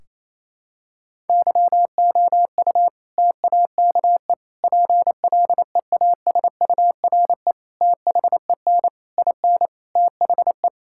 Morse code, heard as a keyed tone, about 28 wpm, 700 Hz. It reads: YOU TAKE PLEASURE THEN IN THE
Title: Sentences from Top 1600 Words 28wpm